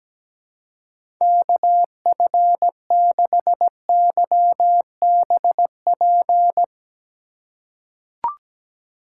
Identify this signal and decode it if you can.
Morse code, heard as a keyed tone, about 17 wpm, 700 Hz. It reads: KF6YBP
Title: USA Call Signs - 3x Character Spacing - 17wpm